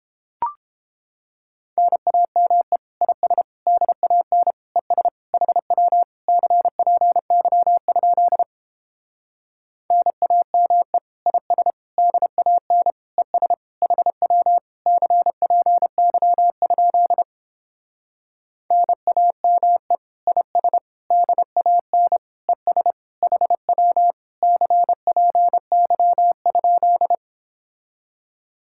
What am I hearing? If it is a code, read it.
Morse code, heard as a keyed tone, about 29 wpm, 700 Hz. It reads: NAME IS DAN ES HW CPY? NAME IS DAN ES HW CPY? NAME IS DAN ES HW CPY?